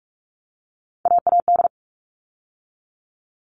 Morse code, heard as a keyed tone, about 44 wpm, 700 Hz. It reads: AAD
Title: All Cut Numbers - 3-Digit 45wpm